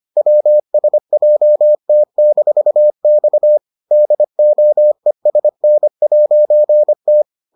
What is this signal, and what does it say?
Morse code, heard as a keyed tone, about 25 wpm, 600 Hz. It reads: WSJT-X DOESN'T